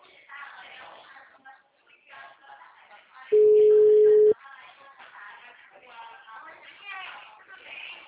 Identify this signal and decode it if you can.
Ringback tone (Europe, 425 Hz), with speech.
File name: out- -2020- - - .139